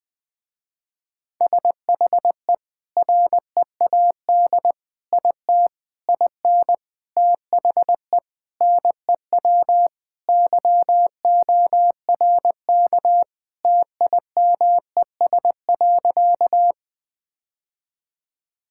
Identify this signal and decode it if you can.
Morse code, heard as a keyed tone, about 20 wpm, 700 Hz. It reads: SHE READ IT IN THE NEW YORK TIMES.
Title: Sentences with 5-Word Phrases - Spoken Beginning - 20wpm